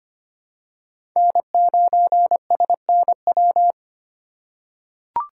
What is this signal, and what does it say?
Morse code, heard as a keyed tone, about 25 wpm, 700 Hz. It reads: N9SNW
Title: USA Call Signs - 8x Character Spacing - 25wpm